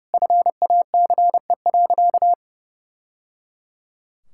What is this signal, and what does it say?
Morse code, heard as a keyed tone, about 30 wpm, 700 Hz. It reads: FACE.
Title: Sentences from Top 1000 Words - Review Part 5 - 30wpm